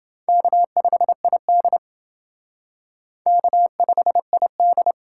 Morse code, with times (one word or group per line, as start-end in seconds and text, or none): 0.3-1.8 K5ID
3.3-4.9 K5ID